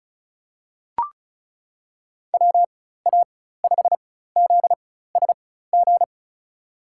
Morse code, heard as a keyed tone, about 36 wpm, spaced as for 16 wpm, 700 Hz. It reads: WA5ZSG
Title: USA Call Signs - 6 Characters - 4x Character Spacing - 35wpm